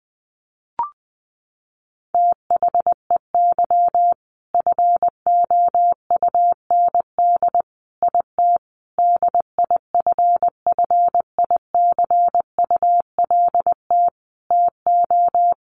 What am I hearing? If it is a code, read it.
Morse code, heard as a keyed tone, about 20 wpm, 700 Hz. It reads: THEY FOUND IT DIFFICULT TO
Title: Sentences from Top 1400 Words 20wpm